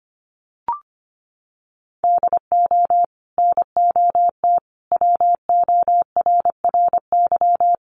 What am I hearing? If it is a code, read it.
Morse code, heard as a keyed tone, about 25 wpm, 700 Hz. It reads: DO NOT WORRY
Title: Sets of 5 Words 25wpm